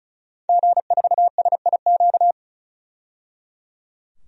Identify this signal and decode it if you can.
Morse code, heard as a keyed tone, about 36 wpm, 700 Hz. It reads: G4SIQ
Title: DXCC-100 Call Signs - 5 Characters - 35wpm